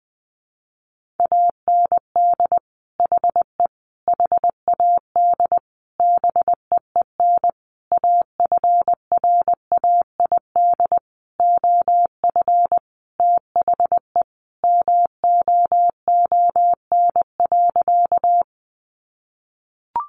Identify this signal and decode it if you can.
Morse code, heard as a keyed tone, about 19 wpm, 700 Hz. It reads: AND HE HAD BEEN AFRAID OF THE MOON.
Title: Sentences from Top 1500 Words 20wpm